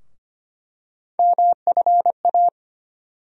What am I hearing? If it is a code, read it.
Morse code, heard as a keyed tone, about 25 wpm, 700 Hz. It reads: MFA
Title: USA Call Sign Postfixes - 3 Letters 25wpm